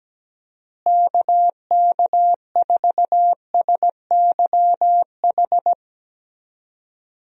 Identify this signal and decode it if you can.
Morse code, heard as a keyed tone, about 17 wpm, 700 Hz. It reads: KK4SYH
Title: USA Call Signs - 6 Characters - Repeated 3x - 17wpm